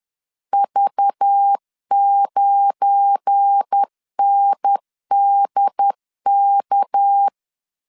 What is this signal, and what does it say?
Morse code, heard as a keyed tone, about 10 wpm, 800 Hz. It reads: V9NDK
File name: LEZIONI / 7 GF9Z / 7 GF9 10wpm 2747